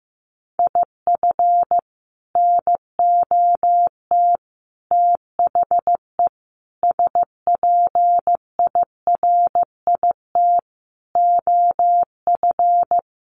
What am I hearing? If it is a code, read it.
Morse code, heard as a keyed tone, about 15 wpm, 700 Hz. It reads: IF NOT THE SPIRIT OF
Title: Sentences from Top 1400 Words 15wpm